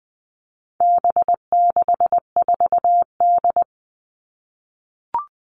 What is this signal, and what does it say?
Morse code, heard as a keyed tone, about 20 wpm, 700 Hz. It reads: B64D
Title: All Cut Numbers - 4-Digit 20wpm